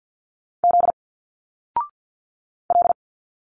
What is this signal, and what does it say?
Morse code, heard as a keyed tone, about 48 wpm, 700 Hz. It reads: Z L
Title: Single Letters 50wpm